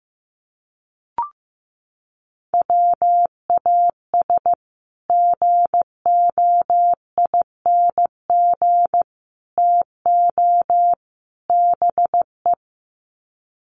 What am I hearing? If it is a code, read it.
Morse code, heard as a keyed tone, about 15 wpm, 700 Hz. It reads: WAS GOING TO BE
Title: Sets of 4 Words 15wpm